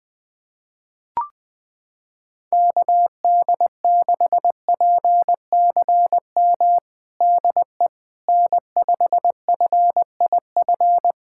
Morse code, heard as a keyed tone, about 21 wpm, 700 Hz. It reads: KD6PCM DE N5FIF
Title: USA Callsign DE Callsign 20wpm